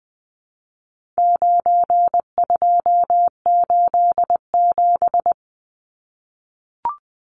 Morse code, heard as a keyed tone, about 20 wpm, 700 Hz. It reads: 9287
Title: Numbers - 4-Digit 20wpm